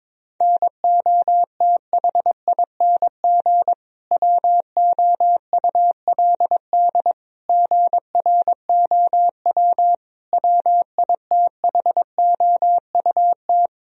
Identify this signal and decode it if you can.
Morse code, heard as a keyed tone, about 22 wpm, 700 Hz. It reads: NOTHING WOULD GROW WITHOUT
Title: Sentences from Top 1000 Words - Review Part 3 - 22wpm